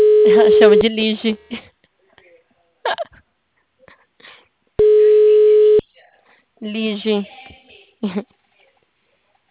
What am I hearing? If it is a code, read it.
Ringback tone (Europe, 425 Hz), with speech.